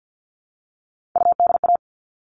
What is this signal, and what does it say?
Morse code, heard as a keyed tone, about 50 wpm, 700 Hz. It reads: UDA
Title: All Cut Numbers - 3-Digit 50wpm